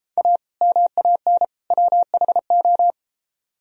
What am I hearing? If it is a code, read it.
Morse code, heard as a keyed tone, about 33 wpm, 700 Hz. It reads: A MAN WHO